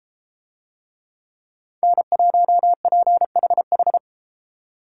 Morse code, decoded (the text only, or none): N1PHH